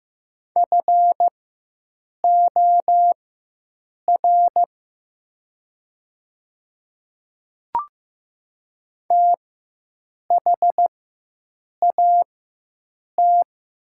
Morse code, heard as a keyed tone, about 15 wpm, spaced as for 7 wpm, 700 Hz. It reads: FOR THAT
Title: 4x Character Spacing - Top 100 Words - 15wpm